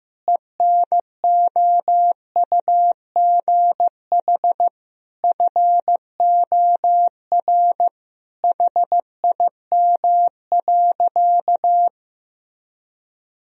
Morse code, heard as a keyed tone, about 15 wpm, 700 Hz. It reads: ENOUGH FOR HIM.
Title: Sentences from Top 1000 Words - Review Part 2 - 15wpm